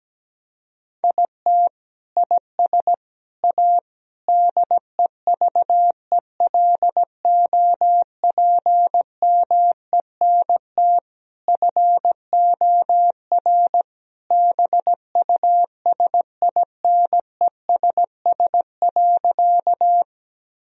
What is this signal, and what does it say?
Morse code, heard as a keyed tone, about 17 wpm, 700 Hz. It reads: IT IS A DEVELOPMENT FOR BUSINESS.
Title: Sentences from Top 1000 Words - Review Part 4 - 17wpm